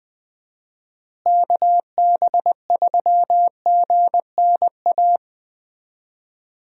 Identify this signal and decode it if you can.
Morse code, heard as a keyed tone, about 20 wpm, 700 Hz. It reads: KB3GNA
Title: USA Call Signs - 6 Characters - Repeated 2x - 20wpm